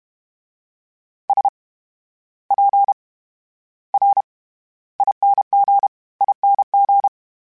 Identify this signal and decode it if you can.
Morse code, heard as a keyed tone, about 32 wpm, spaced as for 15 wpm, 800 Hz. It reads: SPRINGING
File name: Special INGs / Special INGs-mp3s / VerySlowToFaster Ing @30